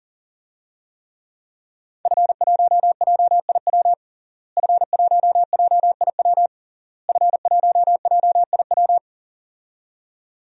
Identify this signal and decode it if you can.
Morse code, heard as a keyed tone, about 40 wpm, 700 Hz. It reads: F1JIW F1JIW F1JIW
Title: DXCC-100 Call Signs - 5 Characters - Repeated 3x - 40wpm